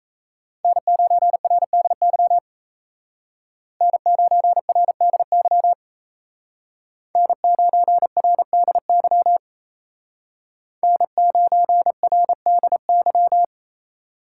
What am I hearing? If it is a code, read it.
Morse code, heard as a keyed tone, about 37 wpm, 700 Hz. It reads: N9RDY N9RDY N9RDY N9RDY